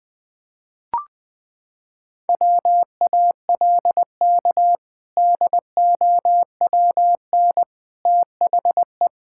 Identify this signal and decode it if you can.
Morse code, heard as a keyed tone, about 20 wpm, 700 Hz. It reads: WALK DOWN THE
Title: Sentences with 2-Word Phrases - 20wpm